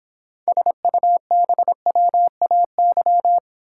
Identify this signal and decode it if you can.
Morse code, heard as a keyed tone, about 26 wpm, 700 Hz. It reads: SUBWAY